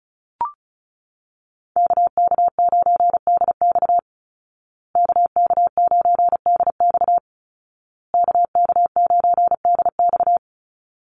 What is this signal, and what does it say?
Morse code, heard as a keyed tone, about 34 wpm, 700 Hz. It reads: KK9DX KK9DX KK9DX
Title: Call Signs Repeated 2x - 35wpm